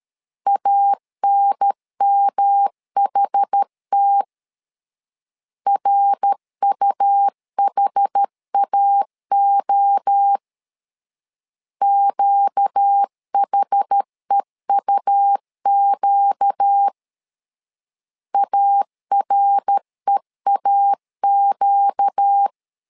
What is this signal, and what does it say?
Morse code, heard as a keyed tone, about 13 wpm, 800 Hz. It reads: ANMHT RUHAO QHEUQ AREAQ
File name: LEZIONI / 4 Q 12wpm 1833